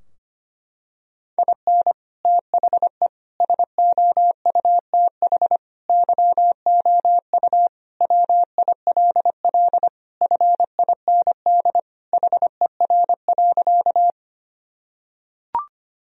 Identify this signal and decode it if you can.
Morse code, heard as a keyed tone, about 25 wpm, 700 Hz. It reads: IN THE SOUTH YOU WILL FIND HER.
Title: Sentences from Top 1000 Words - Review Part 2 - 25wpm